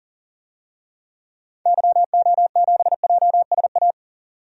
Morse code, heard as a keyed tone, about 40 wpm, 700 Hz. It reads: YO7JSA
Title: DXCC-100 Call Signs - 6 Characters - 40wpm